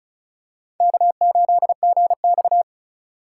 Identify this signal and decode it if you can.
Morse code, heard as a keyed tone, about 34 wpm, 700 Hz. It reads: K8GX
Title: USA Call Signs - 3x Character Spacing - 35wpm